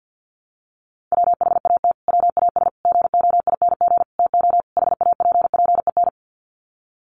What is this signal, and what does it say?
Morse code, heard as a keyed tone, about 50 wpm, 700 Hz. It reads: WHAT WAS GOING TO HAPPEN
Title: Sets of 5 Words Encore 50wpm